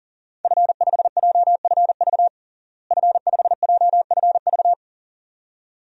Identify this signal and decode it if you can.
Morse code, heard as a keyed tone, about 40 wpm, 700 Hz. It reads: F5JFV F5JFV
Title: DXCC-100 Call Signs - 5 Characters - Repeated 2x - 40wpm